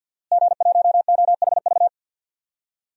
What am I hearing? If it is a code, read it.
Morse code, heard as a keyed tone, about 53 wpm, 700 Hz. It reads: G1OHV
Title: DXCC-100 Call Signs - 5 Characters - Repeated 2x - 50wpm